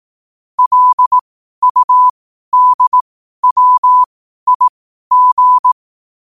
Morse code, heard as a keyed tone, about 18 wpm, spaced as for 12 wpm, 1000 Hz. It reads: LUDWIG
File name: 12wpm